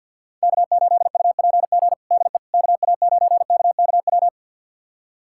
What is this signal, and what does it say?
Morse code, heard as a keyed tone, about 51 wpm, 700 Hz. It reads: K8UPG DE KA9KKW
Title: USA Callsign DE Callsign 50wpm